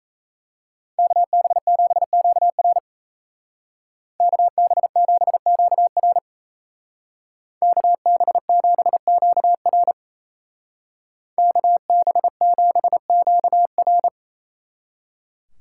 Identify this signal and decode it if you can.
Morse code, heard as a keyed tone, about 36 wpm, 700 Hz. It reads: KB7QR KB7QR KB7QR KB7QR